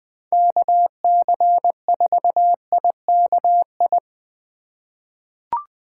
Morse code, heard as a keyed tone, about 20 wpm, 700 Hz. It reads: KC4IKI
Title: USA Call Signs - 8x Character Spacing - 20wpm